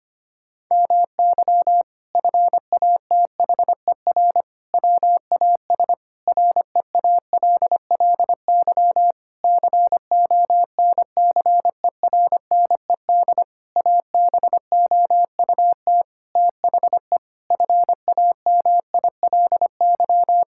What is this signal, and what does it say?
Morse code, heard as a keyed tone, about 25 wpm, 700 Hz. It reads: MY FATHER WAS REALLY CONCERNED ABOUT THE FAMILY
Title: Sentences from Top 1000 Words - Review Part 6 - 25wpm